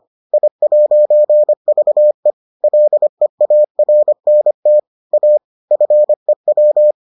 Morse code, heard as a keyed tone, about 25 wpm, 600 Hz. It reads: I'VE LEARNT A FEW